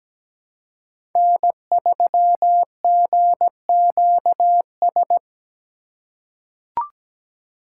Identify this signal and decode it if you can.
Morse code, heard as a keyed tone, about 17 wpm, 700 Hz. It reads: N3GQS
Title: USA Call Signs - 5 Characters - 3x Character Spacing - 17wpm